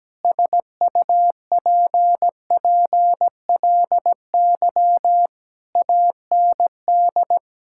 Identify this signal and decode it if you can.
Morse code, heard as a keyed tone, about 17 wpm, 700 Hz. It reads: SUPPLY AND
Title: Sentences from Top 1000 Words - Review Part 5 - 17wpm